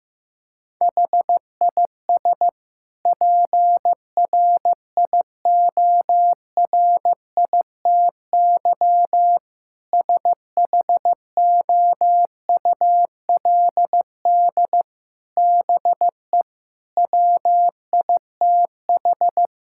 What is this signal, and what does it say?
Morse code, heard as a keyed tone, about 15 wpm, 700 Hz. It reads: HIS PRIORITY SHOULD BE WITH
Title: Sentences from Top 1600 Words 15wpm